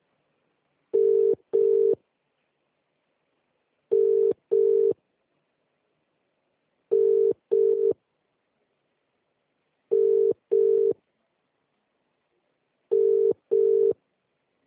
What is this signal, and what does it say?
Ringback tone (UK), 400 + 450 Hz.